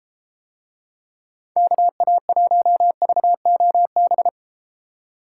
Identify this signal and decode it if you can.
Morse code, heard as a keyed tone, about 33 wpm, 700 Hz. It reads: KA1VOB